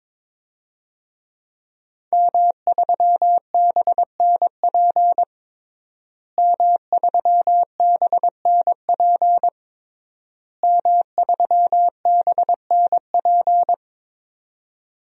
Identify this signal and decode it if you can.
Morse code, heard as a keyed tone, about 22 wpm, 700 Hz. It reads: M3BNP M3BNP M3BNP
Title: DXCC-100 Call Signs - 5 Characters - Repeated 3x - 22wpm